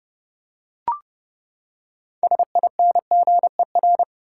Morse code, heard as a keyed tone, about 30 wpm, 700 Hz. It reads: SINGER
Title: Six Letter Words - 30wpm